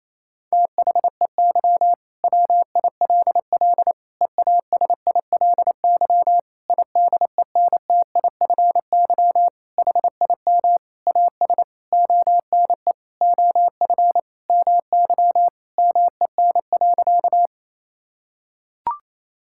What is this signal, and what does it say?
Morse code, heard as a keyed tone, about 28 wpm, 700 Hz. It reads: THEY WILL EASILY IDENTIFY HIM AS ONE OF MY MEN.
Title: Sentences from Top 1000 Words - Review Part 6 - 28wpm